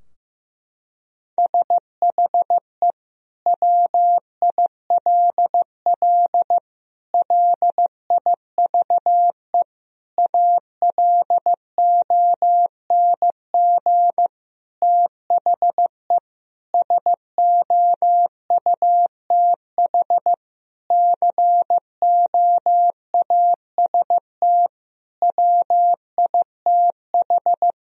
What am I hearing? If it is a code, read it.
Morse code, heard as a keyed tone, about 15 wpm, 700 Hz. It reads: SHE WILL LIVE ALONG THE SOUTH COAST WITH
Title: Sentences from Top 1000 Words - Review Part 4 - 15wpm